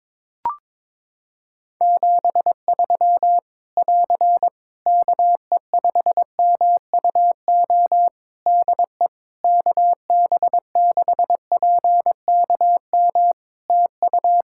Morse code, heard as a keyed tone, about 22 wpm, 700 Hz. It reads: 73 <AR> KE5MUO DE KB6PKM TU
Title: Practice QSO - Exchange 4 - Ending - 22wpm